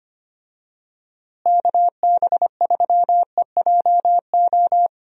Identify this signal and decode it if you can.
Morse code, heard as a keyed tone, about 25 wpm, 700 Hz. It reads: KB3EJO
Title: USA Call Signs - 6 Characters - Repeated 3x - 25wpm